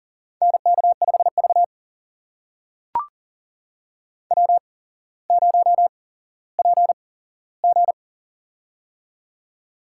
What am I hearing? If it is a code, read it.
Morse code, heard as a keyed tone, about 40 wpm, spaced as for 17 wpm, 700 Hz. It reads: NK5V W0PG
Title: USA Call Signs - 4 Characters - 8x Character Spacing - 40wpm